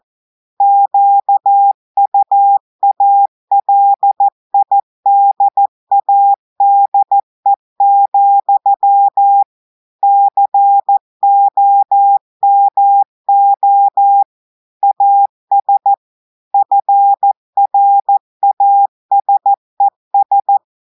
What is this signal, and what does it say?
Morse code, heard as a keyed tone, about 14 wpm, 800 Hz. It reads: QUALIDADE, COMO AS FRASES
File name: audios / portugais / portugais texte en clair 2 vitesse 14 mots minutes